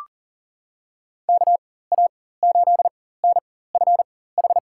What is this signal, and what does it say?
Morse code, heard as a keyed tone, about 38 wpm, spaced as for 18 wpm, 700 Hz. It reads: KA8NFH
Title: USA Call Signs - 6 Characters - 4x Character Spacing - 40wpm